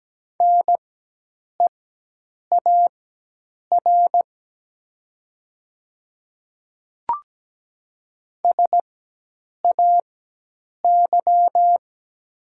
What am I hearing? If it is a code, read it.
Morse code, heard as a keyed tone, about 17 wpm, spaced as for 8 wpm, 700 Hz. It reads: NEAR SAY
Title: 4x Character Spacing - Top 200 Words - 17wpm